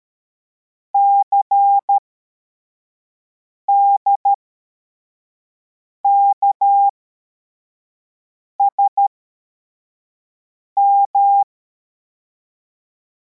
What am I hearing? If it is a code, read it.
Morse code, heard as a keyed tone, about 13 wpm, spaced as for 4 wpm, 800 Hz. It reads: CDKSM